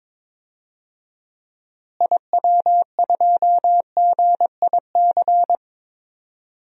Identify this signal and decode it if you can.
Morse code, heard as a keyed tone, about 22 wpm, 700 Hz. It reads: IW2GIC
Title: DXCC-100 Call Signs - 6 Characters - Repeated 2x - 22wpm